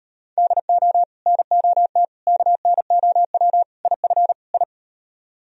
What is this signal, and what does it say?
Morse code, heard as a keyed tone, about 38 wpm, 700 Hz. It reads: DO NOT KNOW IF I